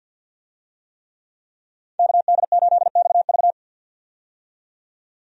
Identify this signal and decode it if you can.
Morse code, heard as a keyed tone, about 50 wpm, 700 Hz. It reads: KD8XV
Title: USA Call Signs - 5 Characters - 50wpm